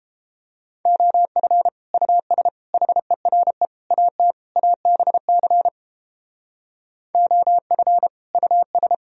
Morse code, heard as a keyed tone, about 31 wpm, 700 Hz. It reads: OF US HERE AT ABC OF US